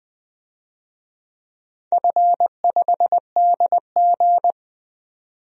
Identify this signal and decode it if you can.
Morse code, heard as a keyed tone, about 20 wpm, 700 Hz. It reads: F5DG
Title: DXCC-100 Call Signs - 4 Characters - 20wpm